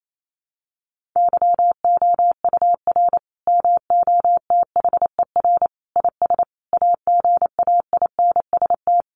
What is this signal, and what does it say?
Morse code, heard as a keyed tone, about 28 wpm, 700 Hz. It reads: YOUR MOTHER IS AGAINST
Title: Sentences from Top 1400 Words 28wpm